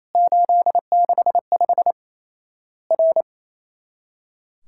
Morse code, heard as a keyed tone, about 28 wpm, 700 Hz. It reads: 865 R
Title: Worked All Europe DX Contest - QTC Exchange - 28wpm